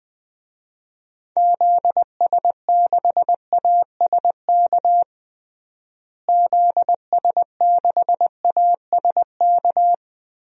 Morse code, heard as a keyed tone, about 20 wpm, 700 Hz. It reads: ZS6ASK ZS6ASK
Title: DXCC-100 Call Signs - 6 Characters - Repeated 3x - 20wpm